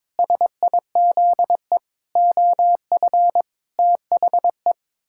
Morse code, heard as a keyed tone, about 22 wpm, 700 Hz. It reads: SIZE OF THE
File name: Sentences with 7-Word Phrases - 22wpm